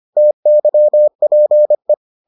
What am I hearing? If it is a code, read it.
Morse code, heard as a keyed tone, about 25 wpm, 600 Hz. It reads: TYPE